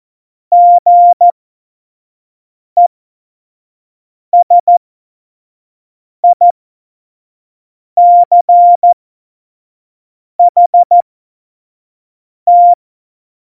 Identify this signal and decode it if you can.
Morse code, heard as a keyed tone, about 14 wpm, spaced as for 5 wpm, 700 Hz. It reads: GESICHT
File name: Morsekurs / Witze / Witz 04. Juni 2019, 25 F70